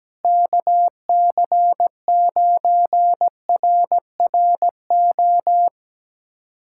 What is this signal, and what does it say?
Morse code, heard as a keyed tone, about 17 wpm, 700 Hz. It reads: KC9RRO
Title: USA Callsign DE Callsign - 6 Characters - 17wpm